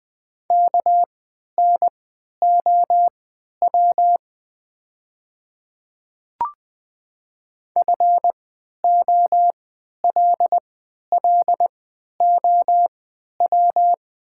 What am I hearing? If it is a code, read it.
Morse code, heard as a keyed tone, about 20 wpm, spaced as for 11 wpm, 700 Hz. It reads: KNOW FOLLOW
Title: 3x Character Spacing - Top 300 Words - 20wpm